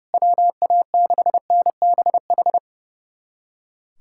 Morse code, heard as a keyed tone, about 30 wpm, 700 Hz. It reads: WA6NBH
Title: USA Callsign DE Callsign - 6 Characters - 30wpm